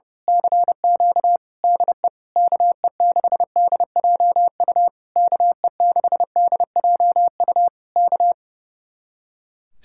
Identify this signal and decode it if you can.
Morse code, heard as a keyed tone, about 28 wpm, 700 Hz. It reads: CQ DE KE6DJU KE6DJU K
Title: Speed-Racing Common QSO Phrases Encore - 30wpm